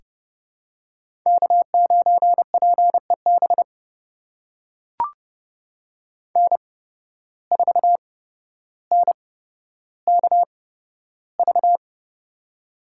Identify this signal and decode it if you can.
Morse code, heard as a keyed tone, about 30 wpm, spaced as for 13 wpm, 700 Hz. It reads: K9PEB N4NKV
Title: USA Call Signs - 5 Characters - 8x Character Spacing - 30wpm